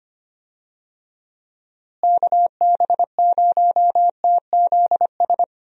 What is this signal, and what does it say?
Morse code, heard as a keyed tone, about 25 wpm, 700 Hz. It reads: KB0TZS
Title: USA Call Signs - 6 Characters - Repeated 3x - 25wpm